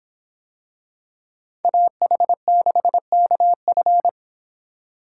Morse code, heard as a keyed tone, about 26 wpm, 700 Hz. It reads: AH6KF